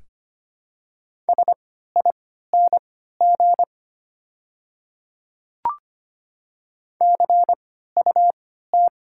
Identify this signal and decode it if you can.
Morse code, heard as a keyed tone, about 25 wpm, spaced as for 14 wpm, 690 Hz. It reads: SING CUT
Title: 3x Character Spacing - Top 500 Words - 25wpm